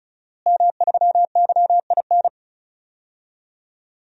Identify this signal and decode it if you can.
Morse code, heard as a keyed tone, about 36 wpm, 700 Hz. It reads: M3YIN
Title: DXCC-100 Call Signs - 5 Characters - Repeated 2x - 35wpm